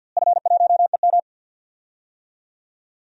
Morse code, heard as a keyed tone, about 52 wpm, 700 Hz. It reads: F1EM
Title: DXCC-100 Call Signs - 4 Characters - Repeated 3x - 50wpm